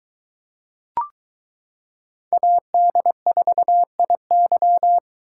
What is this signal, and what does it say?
Morse code, heard as a keyed tone, about 22 wpm, 700 Hz. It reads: AD4IY